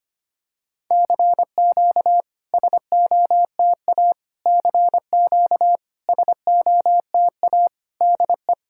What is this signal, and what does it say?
Morse code, heard as a keyed tone, about 25 wpm, 690 Hz. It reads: CQ SOTA CQ SOTA DE
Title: SOTA QSO Practice 25wpm